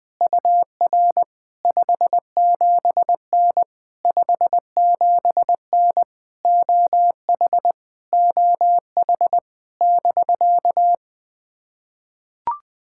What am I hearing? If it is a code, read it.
Morse code, heard as a keyed tone, about 20 wpm, 700 Hz. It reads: UR 57N 57N OH OH <BK>
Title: POTA QSO Practice - 20wpm